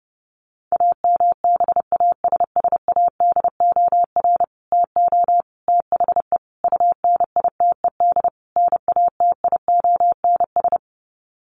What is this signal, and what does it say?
Morse code, heard as a keyed tone, about 30 wpm, 700 Hz. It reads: AMBASSADOR TO THE UNITED NATIONS
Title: Sets of 5 Words Encore2 30wpm